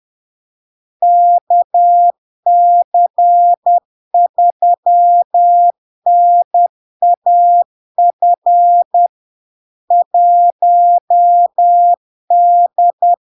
Morse code, heard as a keyed tone, about 10 wpm, 700 Hz. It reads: KC3NAF 1D
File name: LARK-FD-2022-Log-in-CW-10wpm-700-Hz